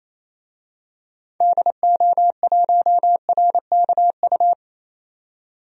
Morse code, heard as a keyed tone, about 28 wpm, 700 Hz. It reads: DO1RKU
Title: DXCC-100 Call Signs - 6 Characters - 28wpm